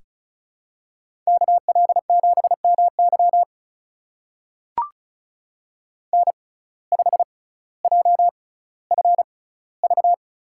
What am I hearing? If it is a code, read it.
Morse code, heard as a keyed tone, about 34 wpm, spaced as for 18 wpm, 700 Hz. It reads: KL7MY N5JFV
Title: USA Call Signs - 5 Characters - 6x Character Spacing - 35wpm